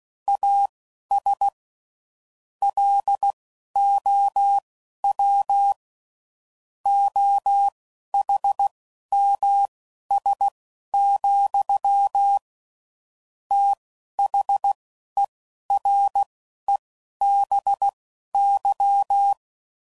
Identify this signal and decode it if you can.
Morse code, heard as a keyed tone, about 16 wpm, spaced as for 11 wpm, 800 Hz. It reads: AS LOW OHMS, THEREBY
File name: CW Code Groups @10WPM / Technical Subjects / XC